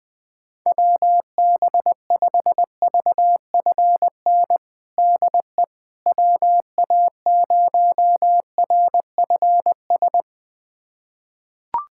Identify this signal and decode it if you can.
Morse code, heard as a keyed tone, about 20 wpm, 700 Hz. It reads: WB5VFN DE WA0RFS
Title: USA Callsign DE Callsign - 6 Characters - 20wpm